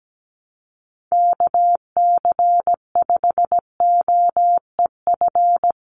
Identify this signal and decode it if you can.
Morse code, heard as a keyed tone, about 17 wpm, 700 Hz. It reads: KC5OEF
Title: USA Call Signs 17wpm